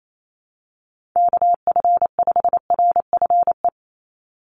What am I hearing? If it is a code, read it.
Morse code, heard as a keyed tone, about 28 wpm, 700 Hz. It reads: KF5RFE
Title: USA Call Signs 28wpm